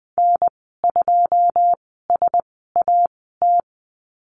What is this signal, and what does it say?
Morse code, heard as a keyed tone, about 20 wpm, spaced as for 14 wpm, 700 Hz. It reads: N2SAT
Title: 088 Lesson 15g - Number 5 - Call Signs 20wpm